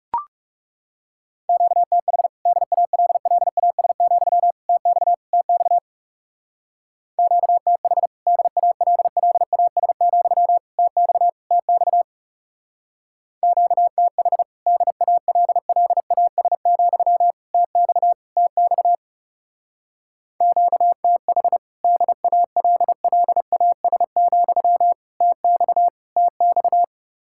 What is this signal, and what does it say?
Morse code, heard as a keyed tone, about 36 wpm, 700 Hz. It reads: QTH DALLAS, TX TX QTH DALLAS, TX TX QTH DALLAS, TX TX QTH DALLAS, TX TX
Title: Speed-Racing Common QSO Phrases Encore - 30wpm